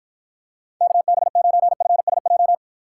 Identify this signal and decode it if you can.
Morse code, heard as a keyed tone, about 54 wpm, 700 Hz. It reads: KB9FSJ